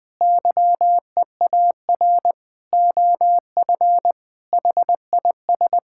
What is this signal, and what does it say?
Morse code, heard as a keyed tone, about 21 wpm, 700 Hz. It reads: YEAR OF HIS
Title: Sets of 8 Words - 20wpm